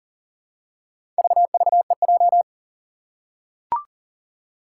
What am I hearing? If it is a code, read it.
Morse code, heard as a keyed tone, about 42 wpm, 700 Hz. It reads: VVEJ
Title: Four Challenging Letter-Number Combinations 40wpm